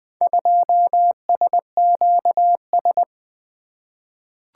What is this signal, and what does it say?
Morse code, heard as a keyed tone, about 20 wpm, 700 Hz. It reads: 2SQS
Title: Four Letter-Number Combinations - 20wpm